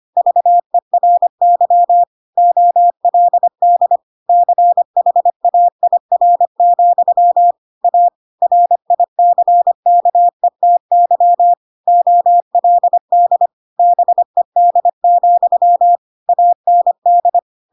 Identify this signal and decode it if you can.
Morse code, heard as a keyed tone, about 25 wpm, 700 Hz. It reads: VERY OLD CHAIR, A RICKETY OLD BED, AND
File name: Pinocchio / Pinocchio ch3 25 25 ik1hge